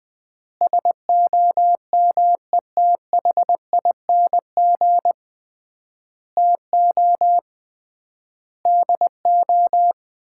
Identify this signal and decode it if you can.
Morse code, heard as a keyed tone, about 20 wpm, 700 Hz. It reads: SOMETHING TO DO
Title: 3x Word-Spacing - Sets of 5 Words - 20wpm